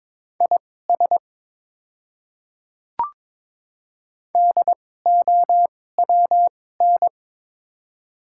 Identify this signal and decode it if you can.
Morse code, heard as a keyed tone, about 22 wpm, spaced as for 16 wpm, 700 Hz. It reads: IS DOWN
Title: 2x Character Spacing - Top 300 Words - 22wpm